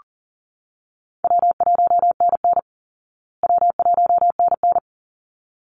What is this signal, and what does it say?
Morse code, heard as a keyed tone, about 40 wpm, 700 Hz. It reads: W1NN W1NN
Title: Call Signs Repeated 3x - 40wpm